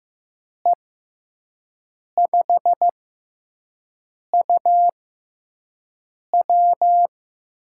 Morse code, heard as a keyed tone, about 15 wpm, spaced as for 5 wpm, 700 Hz. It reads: E5UW
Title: USA Call Signs - 6 Characters - 6x Character Spacing - 15wpm